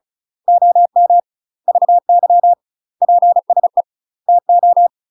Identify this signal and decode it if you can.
Morse code, heard as a keyed tone, about 34 wpm, 700 Hz. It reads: OM VY PSE TO
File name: UK MASTER QSOs TRAINING / 35WPM / UK MASTER QSOs 0001